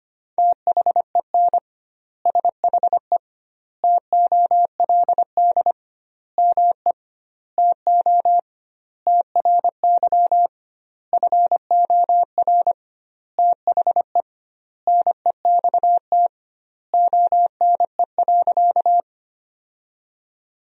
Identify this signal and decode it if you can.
Morse code, heard as a keyed tone, about 25 wpm, 700 Hz. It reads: THEN SHE TOLD ME TO TRY FOR THE NEXT ONE.
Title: 2x Word-Spacing - Sentences from Top 400 Words - 25wpm